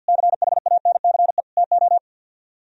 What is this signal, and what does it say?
Morse code, heard as a keyed tone, about 60 wpm, 700 Hz. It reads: CHANCE TO
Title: Speed-Racing - Sets of 3 Words 50wpm